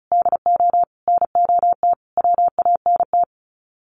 Morse code, heard as a keyed tone, about 34 wpm, 700 Hz. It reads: DO NOT WANT
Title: Sets of 5 Words 35wpm